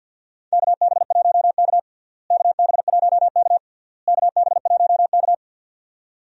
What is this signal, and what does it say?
Morse code, heard as a keyed tone, about 52 wpm, 700 Hz. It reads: KB1K KB1K KB1K
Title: USA Call Signs - 4 Characters - Repeated 3x - 50wpm